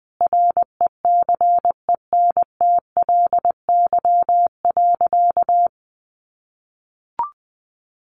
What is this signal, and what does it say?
Morse code, heard as a keyed tone, about 20 wpm, 700 Hz. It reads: RECENTLY.
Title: Sentences from Top 1200 Words 20wpm